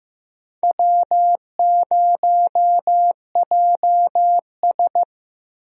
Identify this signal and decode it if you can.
Morse code, heard as a keyed tone, about 15 wpm, 700 Hz. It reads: W0JS
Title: USA Call Signs - 4 Characters - Repeated 3x - 15wpm